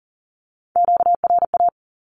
Morse code, heard as a keyed tone, about 38 wpm, 700 Hz. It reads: QRA
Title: Q-Codes with Spoken Meaning - 40wpm